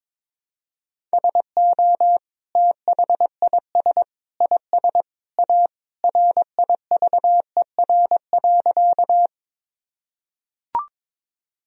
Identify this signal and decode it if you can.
Morse code, heard as a keyed tone, about 22 wpm, 700 Hz. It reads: SO THIS IS A RIVER.
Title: Sentences from Top 1000 Words - Review Part 1 - 22wpm